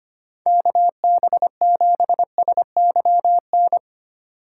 Morse code, heard as a keyed tone, about 25 wpm, 700 Hz. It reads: KB7SYN
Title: USA Call Signs - 6 Characters - 2x Character Spacing - 25wpm